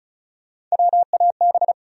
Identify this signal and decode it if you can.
Morse code, heard as a keyed tone, about 36 wpm, 700 Hz. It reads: WAB
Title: USA Call Sign Postfixes - 3 Letters 35wpm